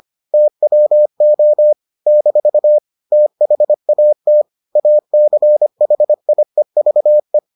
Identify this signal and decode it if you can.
Morse code, heard as a keyed tone, about 25 wpm, 600 Hz. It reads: TWO - THAT ACHIEVE